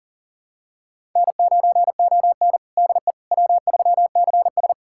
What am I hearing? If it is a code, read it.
Morse code, heard as a keyed tone, about 40 wpm, 700 Hz. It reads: N9ON DE W3CS
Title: USA Callsign DE Callsign - 4 Characters - 40wpm